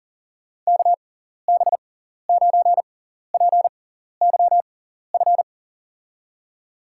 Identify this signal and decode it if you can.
Morse code, heard as a keyed tone, about 40 wpm, spaced as for 14 wpm, 700 Hz. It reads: KB9PYF
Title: USA Call Signs - 6 Characters - 6x Character Spacing - 40wpm